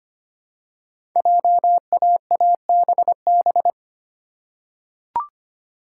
Morse code, heard as a keyed tone, about 25 wpm, 700 Hz. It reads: JAABB
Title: Code Groups - 5 Letters - 25wpm